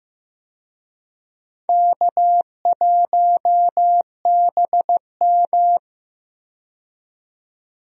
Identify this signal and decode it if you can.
Morse code, heard as a keyed tone, about 15 wpm, 700 Hz. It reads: K1BM
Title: USA Call Signs - 4 Characters - 15wpm